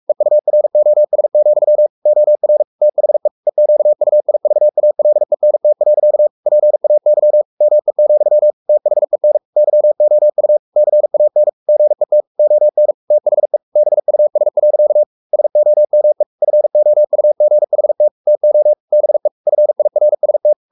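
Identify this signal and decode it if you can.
Morse code, heard as a keyed tone, about 44 wpm, 600 Hz. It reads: EUROS, OR THE EQUIVALENT. PAY ME, THEN YOU CAN GET ON THE BUS. SOME FOUGHT TO BE FIRST
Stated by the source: Morse code at 43 wpm sending Ken Follett's novel Never in English, from CW med Gnister.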